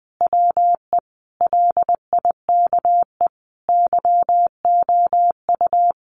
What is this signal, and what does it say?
Morse code, heard as a keyed tone, about 20 wpm, 700 Hz. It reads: WE LIKE YOU
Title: Sentences from Top 1400 Words 20wpm